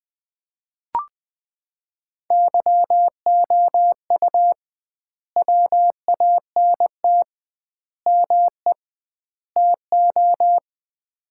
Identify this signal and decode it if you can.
Morse code, heard as a keyed tone, about 20 wpm, 700 Hz. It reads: YOU WANT ME TO
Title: 2x Word-Spacing - Sets of 4 Words - 20wpm